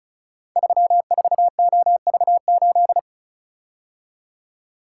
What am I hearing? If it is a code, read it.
Morse code, heard as a keyed tone, about 36 wpm, 690 Hz. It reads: 34OV8
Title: Code Groups - 5 Letter:Numbers - 35wpm